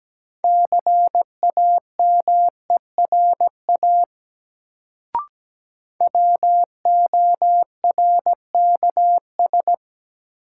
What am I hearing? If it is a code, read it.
Morse code, heard as a keyed tone, about 17 wpm, 700 Hz. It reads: CAMERA WORKS
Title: Sentences from Top 1000 Words - Review Part 5 - New Word Drill - 17wpm